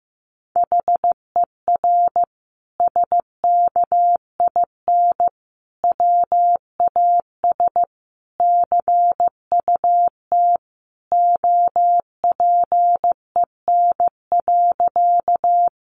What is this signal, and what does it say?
Morse code, heard as a keyed tone, about 14 wpm, 700 Hz. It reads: HER SKIN WAS CUT OPEN.
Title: Sentences from Top 1300 Words 15wpm